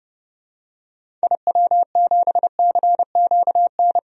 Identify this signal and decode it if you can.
Morse code, heard as a keyed tone, about 30 wpm, 700 Hz. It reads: IW7CQN
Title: DXCC-100 Call Signs - 6 Characters - 30wpm